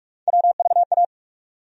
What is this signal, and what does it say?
Morse code, heard as a keyed tone, about 45 wpm, 700 Hz. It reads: WVA
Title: USA Call Sign Postfixes - 3 Letters 45wpm